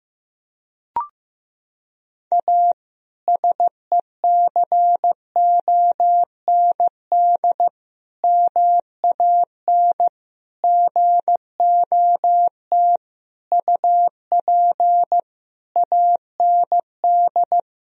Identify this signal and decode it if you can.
Morse code, heard as a keyed tone, about 15 wpm, 700 Hz. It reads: A SECOND MAN GOT UP AND
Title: Sentences from Top 1000 Words - Review Part 1 - 15wpm